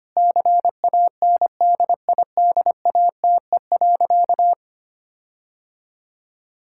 Morse code, heard as a keyed tone, about 25 wpm, 700 Hz. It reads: CANDIDATE.
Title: Sentences from Top 1600 Words 25wpm